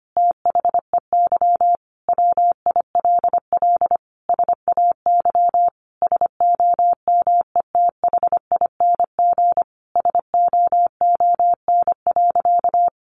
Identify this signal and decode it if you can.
Morse code, heard as a keyed tone, about 25 wpm, 700 Hz. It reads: THEY WILL SAY SOMETHING SOON.
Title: Sentences from Top 1500 Words 25wpm